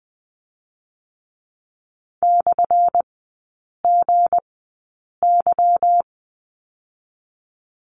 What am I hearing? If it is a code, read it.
Morse code, heard as a keyed tone, about 20 wpm, 700 Hz. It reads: / G Y
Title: 178 Lesson 29e - Slash - Three Characters 20wpm